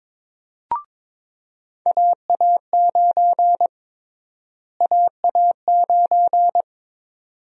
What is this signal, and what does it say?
Morse code, heard as a keyed tone, about 22 wpm, 700 Hz. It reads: AA9 AA9
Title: USA Call Sign Prefixes - Repeated 2x - 22wpm